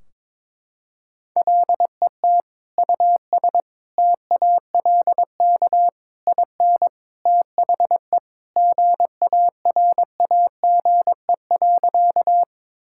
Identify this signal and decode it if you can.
Morse code, heard as a keyed tone, about 21 wpm, 700 Hz. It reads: LET US TALK IN THE GARAGE.
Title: Sentences from Top 1600 Words 22wpm